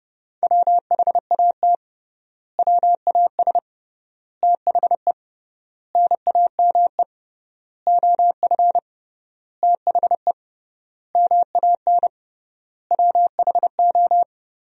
Morse code, heard as a keyed tone, about 30 wpm, 700 Hz. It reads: WHAT WAS THE NAME OF THE MAN WHO
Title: 3x Word-Spacing - Sentences from Top 300 Words - 30wpm